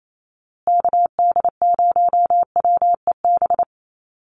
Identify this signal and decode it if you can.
Morse code, heard as a keyed tone, about 28 wpm, 700 Hz. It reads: KD0WEB
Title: USA Call Signs Repeated 2x 28wpm